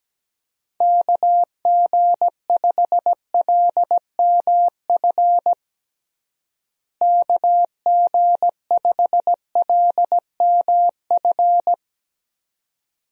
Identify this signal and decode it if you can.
Morse code, heard as a keyed tone, about 17 wpm, 700 Hz. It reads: KG5LMF KG5LMF
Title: USA Call Signs - 6 Characters - Repeated 2x - 17wpm